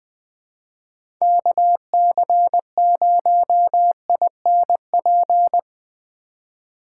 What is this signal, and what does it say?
Morse code, heard as a keyed tone, about 20 wpm, 700 Hz. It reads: KC0INP
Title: K1USN SST QSO Practice - 20wpm